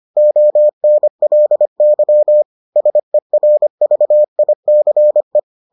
Morse code, heard as a keyed tone, about 25 wpm, 600 Hz. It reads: ONLY SERVICE